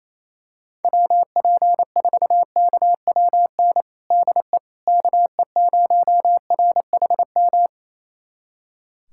Morse code, heard as a keyed tone, about 28 wpm, 700 Hz. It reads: WP4KWN DE KE0RHM
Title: USA Callsign DE Callsign - 6 Characters - 28wpm